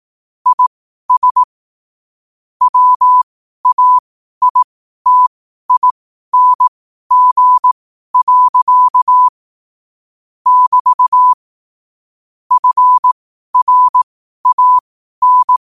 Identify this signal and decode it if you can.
Morse code, heard as a keyed tone, about 18 wpm, spaced as for 12 wpm, 1000 Hz. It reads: IS WAITING. <BT> FRAN
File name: qotdfiles / 12wpm